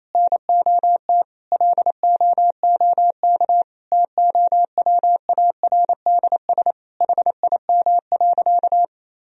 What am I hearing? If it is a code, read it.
Morse code, heard as a keyed tone, about 27 wpm, 700 Hz. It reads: NOT LOOK TOWARDS HIM.
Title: Sentences from Top 1000 Words - Review Part 2 - 28wpm